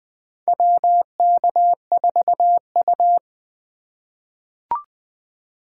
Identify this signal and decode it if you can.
Morse code, heard as a keyed tone, about 20 wpm, 700 Hz. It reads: WK4U
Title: USA Call Signs - 2x Character Spacing - 20wpm